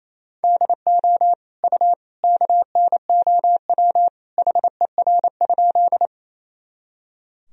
Morse code, heard as a keyed tone, about 28 wpm, 700 Hz. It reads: DO U KNOW HER?
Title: Sentences from Top 100 Words and Abbreviations 28wpm